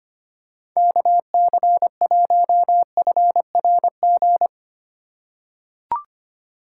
Morse code, heard as a keyed tone, about 25 wpm, 700 Hz. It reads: KC1FRG
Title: USA Call Signs - 6 Characters - 2x Character Spacing - 25wpm